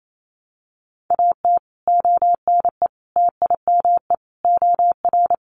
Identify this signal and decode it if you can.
Morse code, heard as a keyed tone, about 28 wpm, 700 Hz. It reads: AT ONE TIME OR
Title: Sets of 5 Words Encore 28wpm